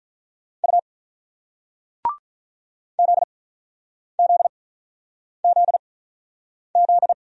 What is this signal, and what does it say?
Morse code, heard as a keyed tone, about 48 wpm, spaced as for 9 wpm, 700 Hz. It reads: U ZZZZ